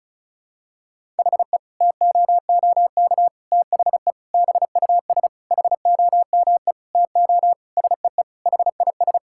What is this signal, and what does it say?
Morse code, heard as a keyed tone, about 34 wpm, 700 Hz. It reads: HE TOOK THE BUS HOME TO SEE HIS
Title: Sentences from Top 1000 Words - Review Part 6 - 35wpm